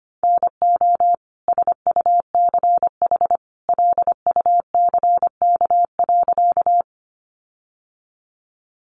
Morse code, heard as a keyed tone, about 25 wpm, 700 Hz. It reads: NO SUCH LUCK.
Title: Sentences from Top 1500 Words 25wpm